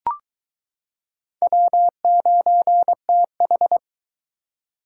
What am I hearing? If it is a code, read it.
Morse code, heard as a keyed tone, about 23 wpm, 700 Hz. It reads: W9TH